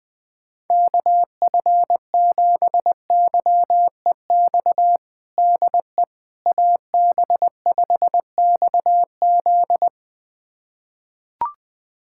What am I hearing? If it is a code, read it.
Morse code, heard as a keyed tone, about 20 wpm, 700 Hz. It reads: KF7YEX DE AB5XZ
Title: USA Callsign DE Callsign 20wpm